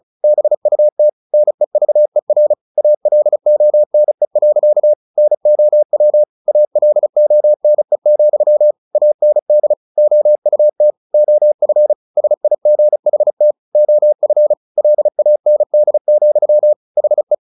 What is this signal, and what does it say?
Morse code, heard as a keyed tone, about 34 wpm, 600 Hz. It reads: BUT NEVER ALONE. NOW ALONE, AND OUT OF SIGHT OF LAND, HE
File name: the old man and the sea / the old man and the sea 1 35wpm